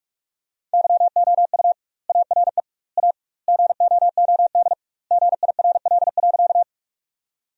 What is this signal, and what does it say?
Morse code, heard as a keyed tone, about 46 wpm, 700 Hz. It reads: YOU ARE A GOOD GIRL.
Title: 2x Word-Spacing - Sentences from Top 400 Words - 45wpm